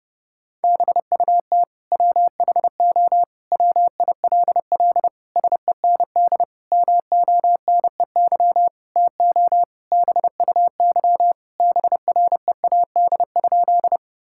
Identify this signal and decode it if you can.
Morse code, heard as a keyed tone, about 30 wpm, 700 Hz. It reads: BUT WHO WILL SEND MONEY TO BUY BREAD?
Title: Sentences from Top 1600 Words 30wpm